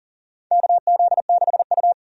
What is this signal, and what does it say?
Morse code, heard as a keyed tone, about 40 wpm, 700 Hz. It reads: KZ6U
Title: USA Call Signs - 4 Characters - 40wpm